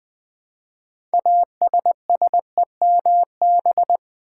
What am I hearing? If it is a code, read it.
Morse code, heard as a keyed tone, about 20 wpm, 700 Hz. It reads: ASSEMB
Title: Build A Word - 3 Letters At A Time - 8-Letter Words - 20wpm